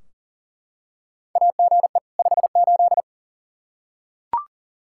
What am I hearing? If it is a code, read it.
Morse code, heard as a keyed tone, about 40 wpm, 700 Hz. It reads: AGE 58
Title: Practice QSO - Exchange 3 - More Info - 40wpm